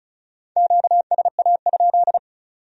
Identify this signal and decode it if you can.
Morse code, heard as a keyed tone, about 46 wpm, 700 Hz. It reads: QSA?